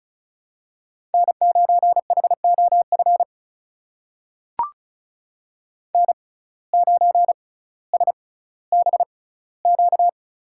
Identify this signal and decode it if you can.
Morse code, heard as a keyed tone, about 36 wpm, spaced as for 18 wpm, 700 Hz. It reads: N9HOF N9SBQ
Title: USA Call Signs - 5 Characters - 6x Character Spacing - 35wpm